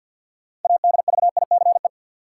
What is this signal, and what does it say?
Morse code, heard as a keyed tone, about 53 wpm, 700 Hz. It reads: ADVICE
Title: Sentences from Top 1100 Words - New Word Drill Alt - 50wpm